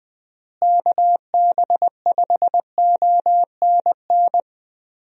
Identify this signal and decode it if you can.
Morse code, heard as a keyed tone, about 20 wpm, 700 Hz. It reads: KB5ONN
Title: USA Call Signs - 6 Characters - Repeated 3x - 20wpm